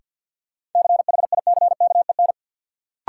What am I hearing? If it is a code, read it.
Morse code, heard as a keyed tone, about 53 wpm, 700 Hz. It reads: CHICKEN
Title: Sentences from Top 1000 Words - Review Part 5 - New Word Drill - 50wpm